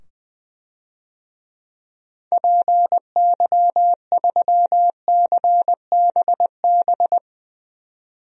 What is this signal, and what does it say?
Morse code, heard as a keyed tone, about 21 wpm, 700 Hz. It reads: PY3CBB
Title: DXCC-100 Call Signs - 6 Characters - 20wpm